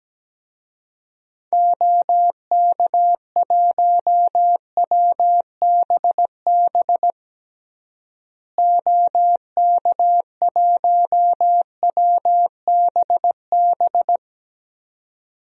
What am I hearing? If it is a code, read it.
Morse code, heard as a keyed tone, about 17 wpm, 700 Hz. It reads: OK1WBB OK1WBB
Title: DXCC-100 Call Signs - 6 Characters - Repeated 2x - 17wpm